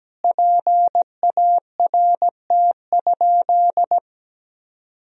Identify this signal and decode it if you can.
Morse code, heard as a keyed tone, about 17 wpm, 700 Hz. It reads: PART?
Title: Sentences from Top 1000 Words - Review Part 4 - 17wpm